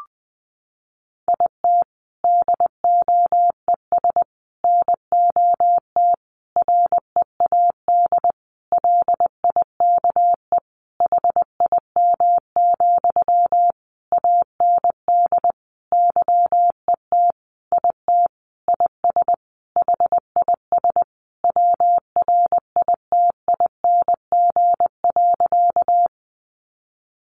Morse code, heard as a keyed tone, about 20 wpm, 700 Hz. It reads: IT DOES NOT READ LIKE HIM, AND YET IT IS HIS WRITING.
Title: Sentences from Top 800 Words 20wpm